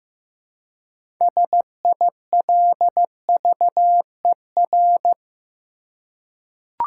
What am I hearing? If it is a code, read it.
Morse code, heard as a keyed tone, about 15 wpm, 700 Hz. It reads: SILVER
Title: Six Letter Words - 15wpm